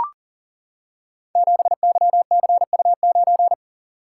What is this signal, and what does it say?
Morse code, heard as a keyed tone, about 42 wpm, 700 Hz. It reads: 7YCU9
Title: Code Groups - 5 Letter:Numbers - 40wpm